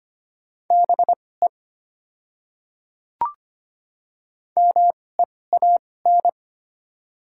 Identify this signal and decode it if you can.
Morse code, heard as a keyed tone, about 25 wpm, spaced as for 18 wpm, 690 Hz. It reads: BE MEAN
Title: 2x Character Spacing - Top 200 Words - 25wpm